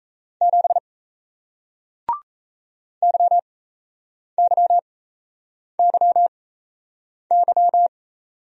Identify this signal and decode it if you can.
Morse code, heard as a keyed tone, about 37 wpm, spaced as for 8 wpm, 700 Hz. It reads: 7 YYYY